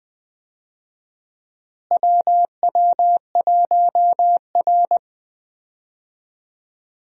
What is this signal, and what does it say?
Morse code, heard as a keyed tone, about 20 wpm, 700 Hz. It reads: WW1R
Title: USA Call Signs - 4 Characters - Repeated 2x - 20wpm